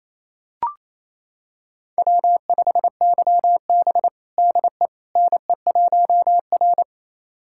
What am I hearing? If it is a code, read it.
Morse code, heard as a keyed tone, about 28 wpm, 700 Hz. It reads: W5YB DE NE1R
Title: USA Callsign DE Callsign - 4 Characters - 28wpm